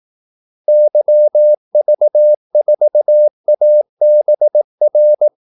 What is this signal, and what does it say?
Morse code, heard as a keyed tone, about 18 wpm, 600 Hz. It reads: YV4ABR
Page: Kaldesignaler 18 wpm